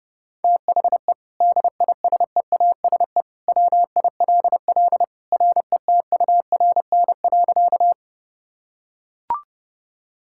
Morse code, heard as a keyed tone, about 28 wpm, 700 Hz. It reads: THE DISEASE WILL RETURN.
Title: Sentences from Top 1000 Words - Review Part 6 - 30wpm